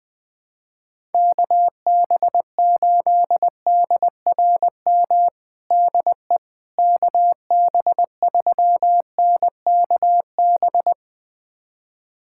Morse code, heard as a keyed tone, about 20 wpm, 700 Hz. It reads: KB8DRM DE KB3NKB
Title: USA Callsign DE Callsign - 6 Characters - 20wpm